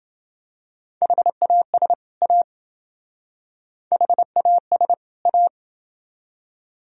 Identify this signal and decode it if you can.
Morse code, heard as a keyed tone, about 28 wpm, 700 Hz. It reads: HAS A HAS A